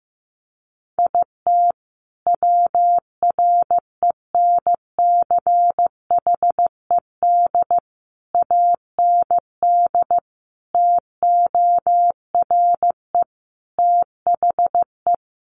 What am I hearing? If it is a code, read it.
Morse code, heard as a keyed tone, about 15 wpm, 700 Hz. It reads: IT WRENCHED AND TORE THE
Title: A Sound Of Thunder 15wpm